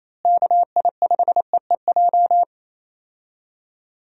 Morse code, heard as a keyed tone, about 27 wpm, 700 Hz. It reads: KI5EEJ
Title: USA Call Signs - 6 Characters - Repeated 3x - 28wpm